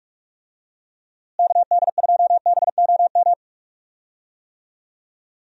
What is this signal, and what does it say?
Morse code, heard as a keyed tone, about 42 wpm, 700 Hz. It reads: KD2BOM
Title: USA Call Signs - 6 Characters - Repeated 3x - 45wpm